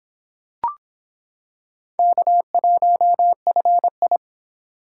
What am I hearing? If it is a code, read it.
Morse code, heard as a keyed tone, about 25 wpm, 700 Hz. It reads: K1FI